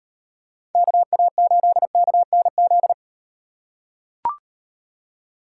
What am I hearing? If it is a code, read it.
Morse code, heard as a keyed tone, about 38 wpm, 700 Hz. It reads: KA8KNZ